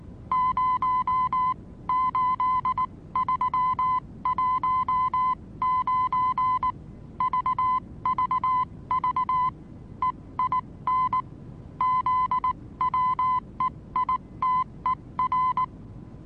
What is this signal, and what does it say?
Morse code, heard as a keyed tone, about 19 wpm, spaced as for 16 wpm, 1020 Hz. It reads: 08319 VVV EIN ZWEITER
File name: repos / Morsekurs / DJ4UF Morsen26